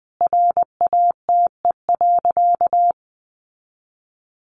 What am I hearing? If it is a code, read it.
Morse code, heard as a keyed tone, about 20 wpm, 700 Hz. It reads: RATE.
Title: Sentences from Top 900 Words 20wpm